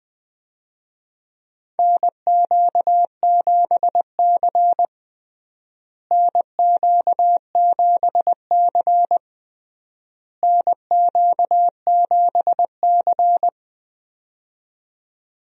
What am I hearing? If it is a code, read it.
Morse code, heard as a keyed tone, about 20 wpm, 700 Hz. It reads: NQ7C NQ7C NQ7C
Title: USA Call Signs - 4 Characters - Repeated 3x - 20wpm